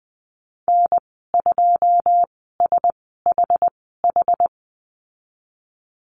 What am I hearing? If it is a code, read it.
Morse code, heard as a keyed tone, about 19 wpm, spaced as for 14 wpm, 700 Hz. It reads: N2SHH
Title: 082 Lesson 14g - Number 2 - Call Signs 20wpm